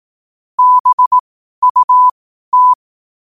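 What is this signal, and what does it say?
Morse code, heard as a keyed tone, about 18 wpm, spaced as for 12 wpm, 1000 Hz. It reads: BUT
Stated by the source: 12 wpm Morse.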